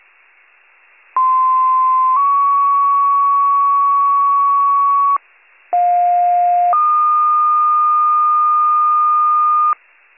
Two-tone page, heard as a one-second tone, then a three-second tone, about 1040 then 1120 Hz, about 710 then 1220 Hz. It is heard from 1.2 to 9.7 s.